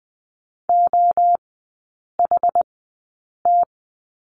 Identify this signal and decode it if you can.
Morse code, heard as a keyed tone, about 20 wpm, spaced as for 8 wpm, 700 Hz. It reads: OHT
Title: 074 Lesson 13e - Letter L - Three Characters 20wpm